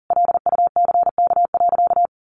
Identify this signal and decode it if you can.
Morse code, heard as a keyed tone, about 40 wpm, 700 Hz. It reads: LUCK.
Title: Sentences from Top 1500 Words 40wpm